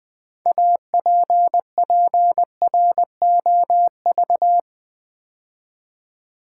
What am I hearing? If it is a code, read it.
Morse code, heard as a keyed tone, about 20 wpm, 700 Hz. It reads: APPROV
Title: Build A Word - 2 Letters At A Time - 8-Letter Words - 20wpm